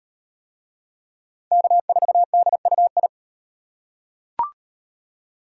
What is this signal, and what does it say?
Morse code, heard as a keyed tone, about 38 wpm, 700 Hz. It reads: K4DUI